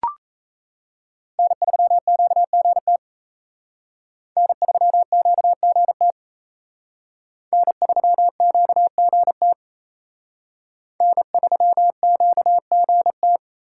Morse code, heard as a keyed tone, about 35 wpm, 700 Hz. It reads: N3QGT N3QGT N3QGT N3QGT
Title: Speed Racing - USA 1x3 Call Signs 28wpm